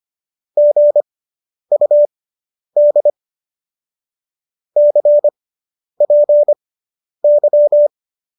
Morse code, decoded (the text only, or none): GUD CPY